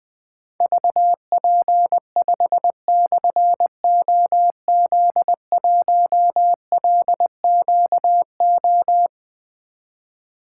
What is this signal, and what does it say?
Morse code, heard as a keyed tone, about 20 wpm, 700 Hz. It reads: VP5/OZ1LQO
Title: Difficult Call Signs 20wpm